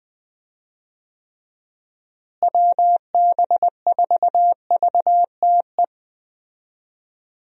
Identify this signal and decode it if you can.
Morse code, heard as a keyed tone, about 20 wpm, 700 Hz. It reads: WB4VTE
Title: USA Call Signs - 6 Characters - Repeated 3x - 20wpm